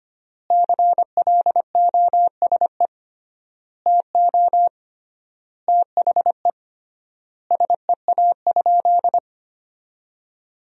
Morse code, heard as a keyed tone, about 25 wpm, 700 Hz. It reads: CLOSE TO THE SEA?
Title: 3x Word-Spacing - Sentences from Top 300 Words - 25wpm